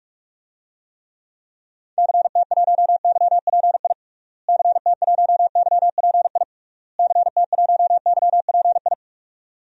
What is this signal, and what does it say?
Morse code, heard as a keyed tone, about 43 wpm, 700 Hz. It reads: CT1YPI CT1YPI CT1YPI
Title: DXCC-100 Call Signs - 6 Characters - Repeated 3x - 45wpm